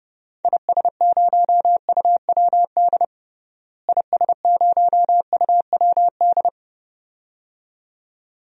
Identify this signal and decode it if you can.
Morse code, heard as a keyed tone, about 30 wpm, 700 Hz. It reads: IS0UWD IS0UWD
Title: DXCC-100 Call Signs - 6 Characters - Repeated 2x - 30wpm